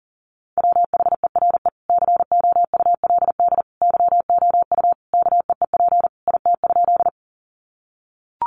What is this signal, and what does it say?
Morse code, heard as a keyed tone, about 40 wpm, 700 Hz. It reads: WHERE COULD YOU KEEP IT?
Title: Sentences from Top 300 Words 40wpm